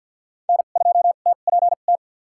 Morse code, heard as a keyed tone, about 48 wpm, spaced as for 35 wpm, 700 Hz. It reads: N2TPT
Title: USA Call Signs - 5 Characters - 2x Character Spacing - 50wpm